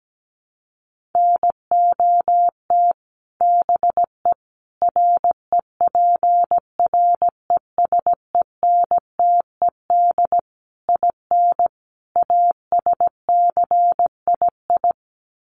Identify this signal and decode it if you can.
Morse code, heard as a keyed tone, about 17 wpm, 700 Hz. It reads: NOT BE REPRESENTED IN ASCII
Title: Sets of 5 Words 17wpm